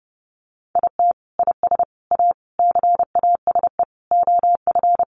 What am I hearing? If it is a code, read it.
Morse code, heard as a keyed tone, about 30 wpm, 700 Hz. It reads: IT IS A CASE OF
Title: Sentences from Top 1400 Words 30wpm